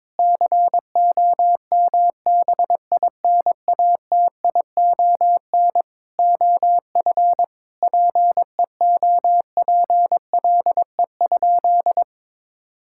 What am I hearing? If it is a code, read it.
Morse code, heard as a keyed tone, about 22 wpm, 700 Hz. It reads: COMBINATION OF PEOPLE?
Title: Sentences from Top 1000 Words - Review Part 6 - 22wpm